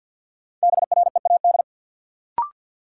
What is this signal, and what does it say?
Morse code, heard as a keyed tone, about 52 wpm, 700 Hz. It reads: BREAD
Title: Sentences from Top 1600 Words - New Word Drill - 50wpm